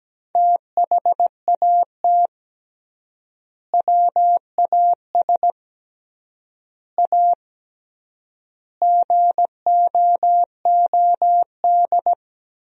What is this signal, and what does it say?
Morse code, heard as a keyed tone, about 17 wpm, 700 Hz. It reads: THAT WAS A GOOD
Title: 3x Word-Spacing - Sentences from Top 200 Words - 17wpm